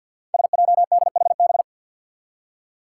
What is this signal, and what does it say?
Morse code, heard as a keyed tone, about 53 wpm, 700 Hz. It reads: SJDHB
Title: Five Challenging Letter-Number Combinations 50wpm